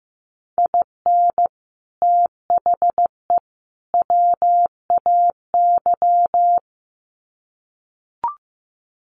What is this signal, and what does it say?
Morse code, heard as a keyed tone, about 15 wpm, 700 Hz. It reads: IN THE WAY
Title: Sets of 3 Words 15wpm